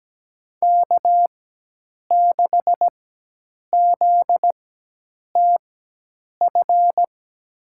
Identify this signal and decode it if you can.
Morse code, heard as a keyed tone, about 17 wpm, spaced as for 8 wpm, 700 Hz. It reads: K6ZTF
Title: USA Call Signs - 5 Characters - 4x Character Spacing - 17wpm